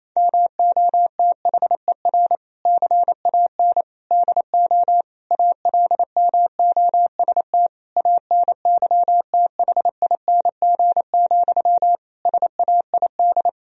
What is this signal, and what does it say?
Morse code, heard as a keyed tone, about 28 wpm, 700 Hz. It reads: MOTHER CAN DO ALMOST ANYTHING, SAID
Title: Sentences from Top 1000 Words - Review Part 4 - 28wpm